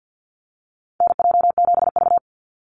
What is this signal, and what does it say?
Morse code, heard as a keyed tone, about 50 wpm, 700 Hz. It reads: NJ7V
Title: NA SOTA Activator Callsigns 50wpm